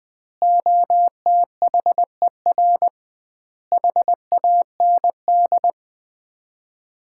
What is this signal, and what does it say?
Morse code, heard as a keyed tone, about 20 wpm, 700 Hz. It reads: OTHER HAND
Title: 2x Word-Spacing - Sets of 5 Words - 20wpm